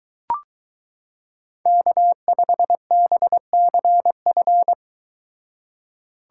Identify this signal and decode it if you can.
Morse code, heard as a keyed tone, about 22 wpm, 700 Hz. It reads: K5BCF